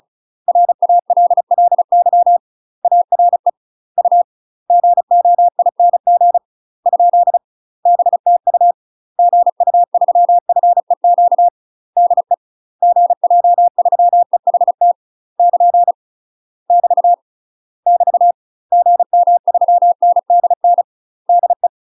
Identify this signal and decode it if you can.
Morse code, heard as a keyed tone, about 34 wpm, 700 Hz. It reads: RALLY ARE U GOING ? BTU GU3FEQ DE GJ3EHT <KN> <BT> <BT> GM3NDN DE
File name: UK MASTER QSOs TRAINING / 35WPM / UK MASTER QSOs 0001